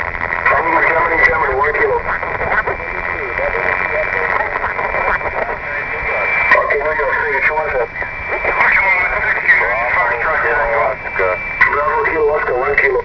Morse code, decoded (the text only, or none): none